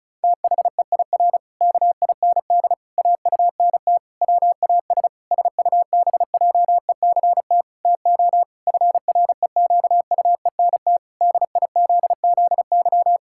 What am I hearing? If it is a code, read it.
Morse code, heard as a keyed tone, about 36 wpm, 700 Hz. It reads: THEIR KIND AUNT WAS SUBJECT TO FREQUENT DIZZY
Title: Pangram Challenge v2 - 35wpm